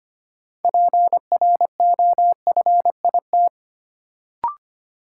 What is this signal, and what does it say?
Morse code, heard as a keyed tone, about 25 wpm, 700 Hz. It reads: PROFIT
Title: Sentences from Top 800 Words - New Word Drill - 25wpm